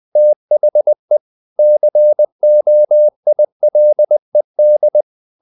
Morse code, heard as a keyed tone, about 20 wpm, 600 Hz. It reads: THE COILED
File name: the old man and the sea / the old man and the sea part 1 20wpm